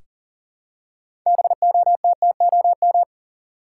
Morse code, heard as a keyed tone, about 38 wpm, 700 Hz. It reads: BOTTOM
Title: Six Letter Words - 40wpm